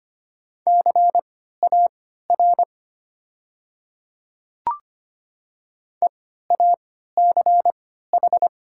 Morse code, heard as a keyed tone, about 25 wpm, spaced as for 14 wpm, 690 Hz. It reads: CAR EACH
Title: 3x Character Spacing - Top 300 Words - 25wpm